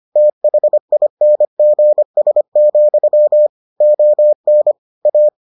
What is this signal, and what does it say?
Morse code, heard as a keyed tone, about 25 wpm, 600 Hz. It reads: THINGS, ON A